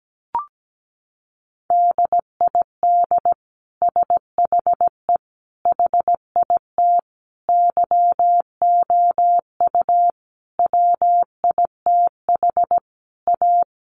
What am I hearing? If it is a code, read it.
Morse code, heard as a keyed tone, about 17 wpm, 700 Hz. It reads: DID SHE HIT YOU WITH A
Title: Sentences from Top 1400 Words 17wpm